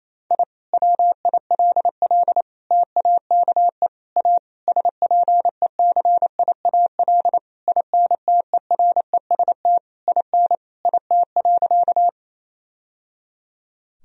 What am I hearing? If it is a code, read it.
Morse code, heard as a keyed tone, about 28 wpm, 700 Hz. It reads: I WILL TAKE A SPECIAL INTEREST IN IT.
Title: Sentences from Top 1000 Words - Review Part 3 - 28wpm